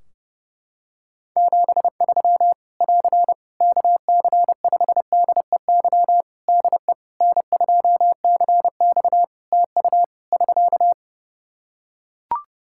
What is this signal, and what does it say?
Morse code, heard as a keyed tone, about 31 wpm, 700 Hz. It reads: 73 <AR> KC5DEY DE N2CX TU <SK>
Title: Practice QSO - Exchange 4 - Ending - 30wpm